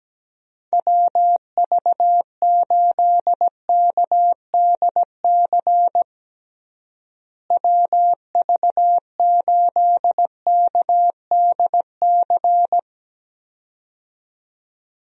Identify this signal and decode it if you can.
Morse code, heard as a keyed tone, about 17 wpm, 700 Hz. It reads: WV8KDC WV8KDC
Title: USA Call Signs - 6 Characters - Repeated 2x - 17wpm